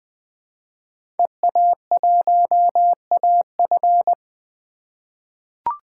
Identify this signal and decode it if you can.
Morse code, heard as a keyed tone, about 20 wpm, 700 Hz. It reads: EA1AF
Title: 2023 POTA Hunter Call Signs - 20wpm